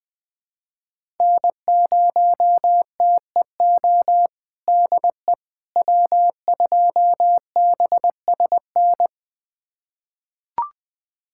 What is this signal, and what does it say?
Morse code, heard as a keyed tone, about 20 wpm, 700 Hz. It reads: N0TEO DE W2BSN
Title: USA Callsign DE Callsign - 5 Characters - 20wpm